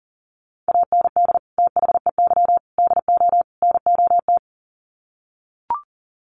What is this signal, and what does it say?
Morse code, heard as a keyed tone, about 40 wpm, 700 Hz. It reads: AND THEY DO NOT
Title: Sets of 4 Words 40wpm